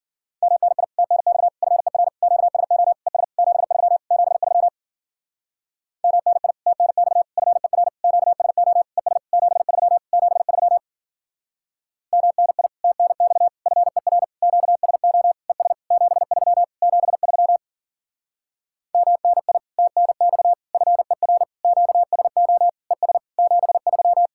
Morse code, heard as a keyed tone, about 50 wpm, 700 Hz. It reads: MNI TNX FER QSO ES 73 73 MNI TNX FER QSO ES 73 73 MNI TNX FER QSO ES 73 73 MNI TNX FER QSO ES 73
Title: Speed-Racing Common QSO Phrases - 40wpm